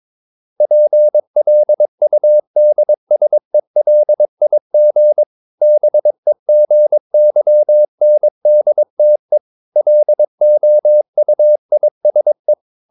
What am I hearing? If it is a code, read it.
Morse code, heard as a keyed tone, about 22 wpm, 600 Hz. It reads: PLUDSELIG BEGYNDTE LOUISE
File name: Ferie QRS 22wpm